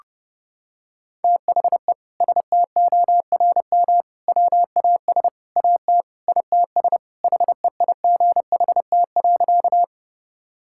Morse code, heard as a keyed tone, about 30 wpm, 700 Hz. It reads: THE STORM WAS AT ITS HEIGHT.
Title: Sentences from Top 1600 Words 30wpm